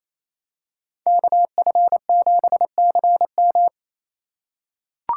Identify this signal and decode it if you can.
Morse code, heard as a keyed tone, about 28 wpm, 700 Hz. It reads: KF7CM
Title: USA Call Signs - 5 Characters - 2x Character Spacing - 28wpm